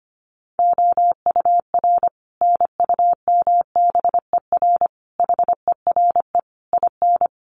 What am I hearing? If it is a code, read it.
Morse code, heard as a keyed tone, about 25 wpm, 700 Hz. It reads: OUR NUMBER HERE IN
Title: Sets of 5 Words Encore2 25wpm